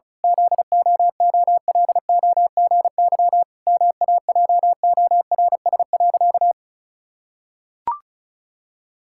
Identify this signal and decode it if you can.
Morse code, heard as a keyed tone, about 34 wpm, 700 Hz. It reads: ZOOLOGY MAJORS.
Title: Pangram Challenge v2 - 35wpm